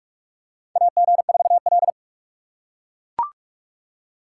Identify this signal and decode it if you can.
Morse code, heard as a keyed tone, about 42 wpm, 700 Hz. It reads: AG4L
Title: USA Call Signs - 4 Characters - 2x Character Spacing - 45wpm